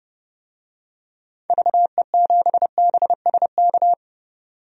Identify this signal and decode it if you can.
Morse code, heard as a keyed tone, about 30 wpm, 700 Hz. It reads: VE7BSK
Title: DXCC-100 Call Signs - 6 Characters - 30wpm